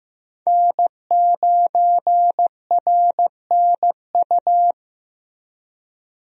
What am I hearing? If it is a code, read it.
Morse code, heard as a keyed tone, about 15 wpm, 700 Hz. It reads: N9RNU
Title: USA Call Signs - 5 Characters - Repeated 2x - 15wpm